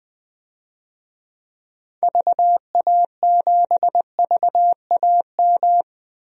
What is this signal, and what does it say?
Morse code, heard as a keyed tone, about 20 wpm, 700 Hz. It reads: VA7VAM
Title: DXCC-100 Call Signs - 6 Characters - Repeated 2x - 20wpm